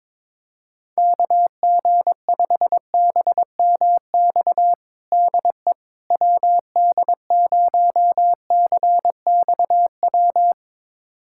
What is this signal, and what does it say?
Morse code, heard as a keyed tone, about 22 wpm, 700 Hz. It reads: KG5BMX DE WD0CXW
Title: USA Callsign DE Callsign - 6 Characters - 22wpm